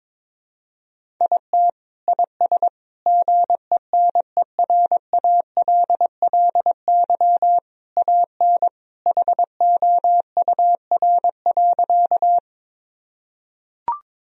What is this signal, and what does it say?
Morse code, heard as a keyed tone, about 22 wpm, 700 Hz. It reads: IT IS GENERALLY AN HOUR.
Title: Sentences from Top 1000 Words - Review Part 4 - 22wpm